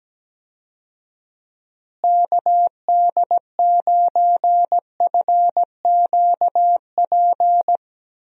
Morse code, heard as a keyed tone, about 17 wpm, 700 Hz. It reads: KD9FQP
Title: USA Call Signs - 6 Characters - Repeated 2x - 17wpm